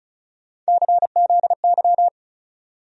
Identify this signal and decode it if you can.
Morse code, heard as a keyed tone, about 36 wpm, 700 Hz. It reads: CZY
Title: USA Call Sign Postfixes - 3 Letters 35wpm